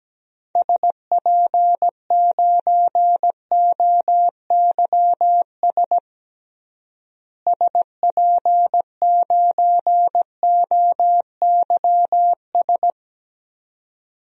Morse code, heard as a keyed tone, about 17 wpm, 700 Hz. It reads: SP9OYS SP9OYS
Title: DXCC-100 Call Signs - 6 Characters - Repeated 2x - 17wpm